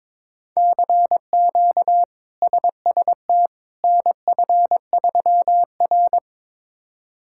Morse code, heard as a keyed tone, about 21 wpm, 700 Hz. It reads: CQ SST NF3R
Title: K1USN SST QSO Practice - 22wpm